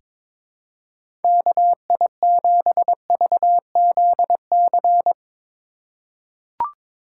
Morse code, heard as a keyed tone, about 22 wpm, 700 Hz. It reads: KI7VZC
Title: USA Call Signs - 6 Characters - 3x Character Spacing - 22wpm